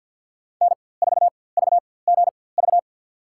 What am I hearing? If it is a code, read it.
Morse code, heard as a keyed tone, about 52 wpm, spaced as for 23 wpm, 700 Hz. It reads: N4VGV
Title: USA Call Signs - 4x Character Spacing - 50wpm